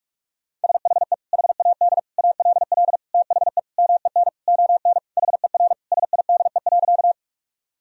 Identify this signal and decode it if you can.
Morse code, heard as a keyed tone, about 44 wpm, 700 Hz. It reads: SHE HAD ALL THE MEN ON HER SIDE.
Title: Sentences from Top 1000 Words - Review Part 1 - 45wpm